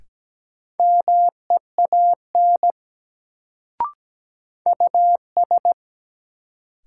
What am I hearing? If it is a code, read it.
Morse code, heard as a keyed tone, about 17 wpm, 700 Hz. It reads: MEAN US
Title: Sets of 2 Words - New Word Drill - 17wpm